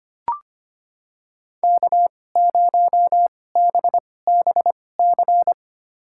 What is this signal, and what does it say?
Morse code, heard as a keyed tone, about 25 wpm, spaced as for 18 wpm, 700 Hz. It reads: K0BBC
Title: USA Call Signs - 2x Character Spacing - 25wpm